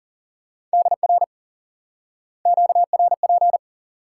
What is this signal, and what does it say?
Morse code, heard as a keyed tone, about 40 wpm, 700 Hz. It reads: DR QRP
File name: IQE Course 3 - Lesson 89 - YAGI - Two Elements - 40wpm